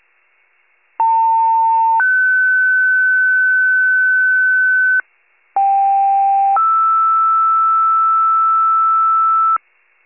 Two-tone page, heard as a one-second tone, then a three-second tone, about 900 then 1500 Hz, about 790 then 1320 Hz, with noise. It is heard from 1.0 to 9.6 s.